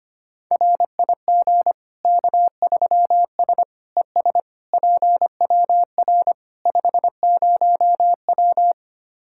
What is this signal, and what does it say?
Morse code, heard as a keyed tone, about 25 wpm, 700 Hz. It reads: RIG K3S ES PWR 50W
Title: K1USN SST QSO Practice 25wpm